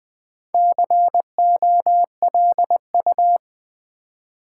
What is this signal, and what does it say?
Morse code, heard as a keyed tone, about 20 wpm, 700 Hz. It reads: COLU
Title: Build A Word - 2 Letters At A Time - 6-Letter Words - 20wpm